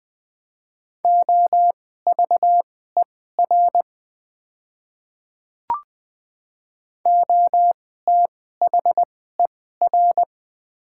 Morse code, heard as a keyed tone, about 20 wpm, spaced as for 14 wpm, 700 Hz. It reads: OVER OTHER
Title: 2x Character Spacing - Top 200 Words - 20wpm